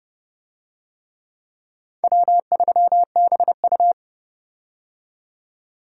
Morse code, heard as a keyed tone, about 30 wpm, 700 Hz. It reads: W3BU
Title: USA Call Signs - 4 Characters - 30wpm